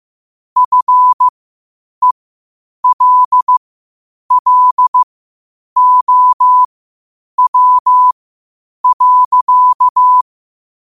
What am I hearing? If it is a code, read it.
Morse code, heard as a keyed tone, about 15 wpm, spaced as for 8 wpm, 1000 Hz. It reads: FELLOW.